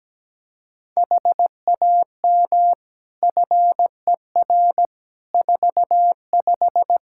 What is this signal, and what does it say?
Morse code, heard as a keyed tone, about 17 wpm, 700 Hz. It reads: HAM FER 45
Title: Practice QSO - Exchange 3 - More Info - 17wpm